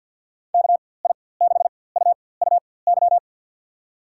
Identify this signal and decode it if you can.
Morse code, heard as a keyed tone, about 47 wpm, spaced as for 23 wpm, 700 Hz. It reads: KI6UUY
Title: USA Call Signs - 6 Characters - 4x Character Spacing - 50wpm